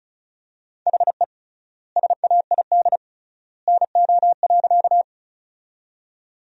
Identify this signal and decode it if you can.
Morse code, heard as a keyed tone, about 36 wpm, 700 Hz. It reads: HE SAID NO.
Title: 3x Word-Spacing - Sentences from Top 100 Words - 35wpm